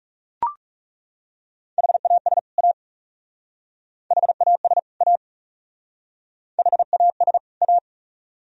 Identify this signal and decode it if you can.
Morse code, heard as a keyed tone, about 35 wpm, 700 Hz. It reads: HAS A HAS A HAS A
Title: Speed-Racing - Sets of 2 Words 30wpm